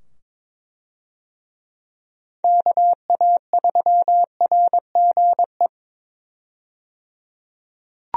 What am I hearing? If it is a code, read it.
Morse code, heard as a keyed tone, about 22 wpm, 700 Hz. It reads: KA3RGE
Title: USA Call Signs - 6 Characters - 22wpm